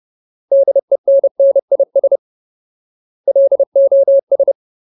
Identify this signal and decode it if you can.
Morse code, heard as a keyed tone, about 30 wpm, 550 Hz. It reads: DENNIS LOS
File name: Ultimate Word List / 30WPM Ultimate